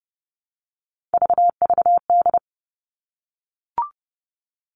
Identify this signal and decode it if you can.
Morse code, heard as a keyed tone, about 30 wpm, 700 Hz. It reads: VVD
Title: Three Challenging Letter-Number Combinations 30wpm